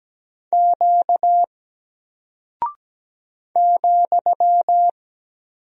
Morse code, heard as a keyed tone, about 17 wpm, 700 Hz. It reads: Q ,
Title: Single Character (Letter:Number:Punctuation:Prosign) Alt 17wpm